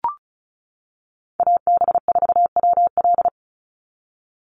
Morse code, heard as a keyed tone, about 34 wpm, 700 Hz. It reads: AB4WL
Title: NA SOTA Chaser Callsigns 35wpm